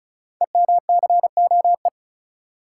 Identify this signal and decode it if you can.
Morse code, heard as a keyed tone, about 36 wpm, 700 Hz. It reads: EMCOE
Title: Code Groups - 5 Letter:Numbers - 35wpm